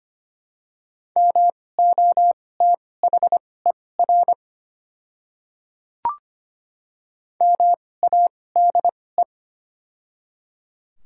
Morse code, heard as a keyed tone, about 25 wpm, spaced as for 18 wpm, 690 Hz. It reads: MOTHER MADE
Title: 2x Character Spacing - Top 200 Words - 25wpm